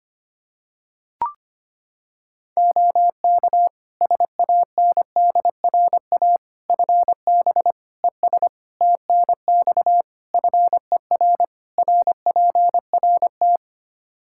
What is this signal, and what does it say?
Morse code, heard as a keyed tone, about 25 wpm, 700 Hz. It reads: OK SANDRA FB ES TNX FER RPRT
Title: Practice QSO - Exchange 2 - Info - 25wpm